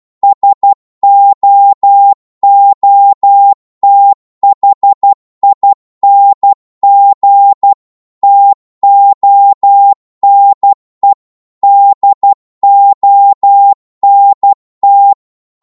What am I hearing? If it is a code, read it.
Morse code, heard as a keyed tone, about 12 wpm, 800 Hz. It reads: SOOTHING TONE DONT
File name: Morse / psg-12wpm